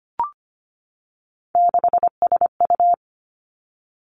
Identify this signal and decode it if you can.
Morse code, heard as a keyed tone, about 25 wpm, 700 Hz. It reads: 6SU
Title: Three Challenging Letter-Number Combinations 25wpm